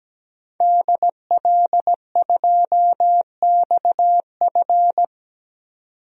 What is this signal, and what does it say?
Morse code, heard as a keyed tone, about 17 wpm, 700 Hz. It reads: DL2XF
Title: DXCC-100 Call Signs - 5 Characters - 17wpm